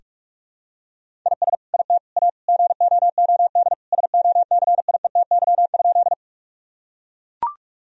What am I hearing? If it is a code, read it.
Morse code, heard as a keyed tone, about 41 wpm, 700 Hz. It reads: IS IT A GOOD SOCIETY?
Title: Sentences from Top 1000 Words - Review Part 4 - 45wpm